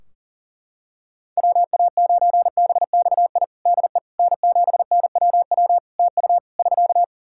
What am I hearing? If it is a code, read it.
Morse code, heard as a keyed tone, about 40 wpm, 700 Hz. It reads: WA9BXI DE N7NWW TU <SK>
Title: Practice QSO - Exchange 4 - Ending - 40wpm